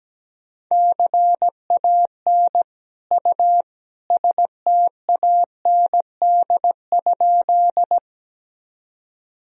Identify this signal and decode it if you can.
Morse code, heard as a keyed tone, about 17 wpm, 700 Hz. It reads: CAN U STAND?
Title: Sentences from Top 300 Words and Abbreviations 17wpm